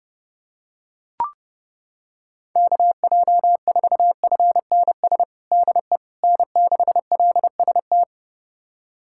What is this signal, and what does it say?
Morse code, heard as a keyed tone, about 32 wpm, 700 Hz. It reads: KJ4FNS DE N6LST
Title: USA Callsign DE Callsign 30wpm